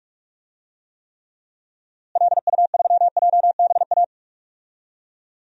Morse code, heard as a keyed tone, about 46 wpm, 700 Hz. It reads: LU3JBA
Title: DXCC-100 Call Signs - 6 Characters - Repeated 2x - 45wpm